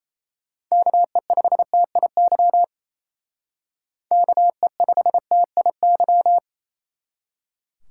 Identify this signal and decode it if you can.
Morse code, heard as a keyed tone, about 34 wpm, 700 Hz. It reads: KE5TIY KE5TIY
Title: Speed Racing - USA Call Signs 28wpm